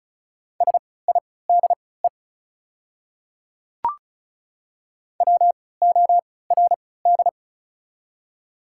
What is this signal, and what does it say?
Morse code, heard as a keyed tone, about 33 wpm, spaced as for 19 wpm, 700 Hz. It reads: SIDE WORD
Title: 3x Character Spacing - Top 300 Words - 35wpm